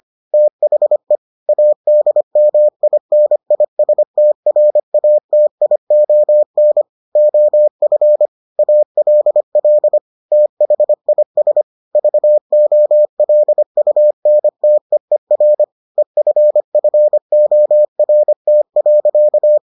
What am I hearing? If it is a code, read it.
Morse code, heard as a keyed tone, about 25 wpm, 600 Hz. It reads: THE ADMINISTRATION OF ALL THIS VOLUNTEER EFFORT.